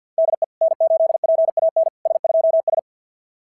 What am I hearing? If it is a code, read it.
Morse code, heard as a keyed tone, about 50 wpm, 660 Hz. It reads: DE N8PAN S2S
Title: SOTA QSO Practice 50wpm